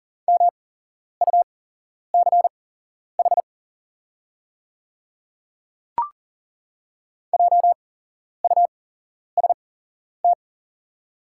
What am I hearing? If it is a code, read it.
Morse code, heard as a keyed tone, about 40 wpm, spaced as for 11 wpm, 700 Hz. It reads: MUCH JUST
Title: 8x Character Spacing - Top 500 Words - 40wpm